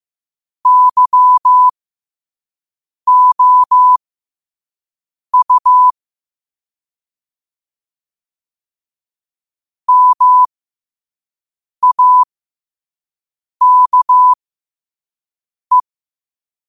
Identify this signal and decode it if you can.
Morse code, heard as a keyed tone, about 15 wpm, spaced as for 5 wpm, 1000 Hz. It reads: YOU MAKE
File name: qotdfiles / 5wpm